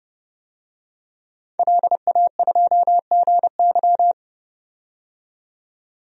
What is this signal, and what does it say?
Morse code, heard as a keyed tone, about 30 wpm, 700 Hz. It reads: LA2GY
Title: DXCC-100 Call Signs - 5 Characters - 30wpm